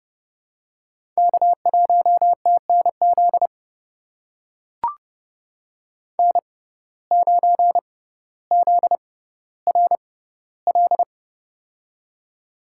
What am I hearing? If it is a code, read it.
Morse code, heard as a keyed tone, about 30 wpm, spaced as for 15 wpm, 700 Hz. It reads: K1TNZ N9ZRL
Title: USA Call Signs - 5 Characters - 6x Character Spacing - 30wpm